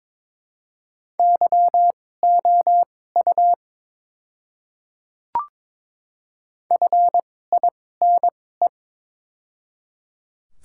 Morse code, heard as a keyed tone, about 22 wpm, spaced as for 16 wpm, 700 Hz. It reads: YOU FINE
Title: 2x Character Spacing - Top 500 Words - 22wpm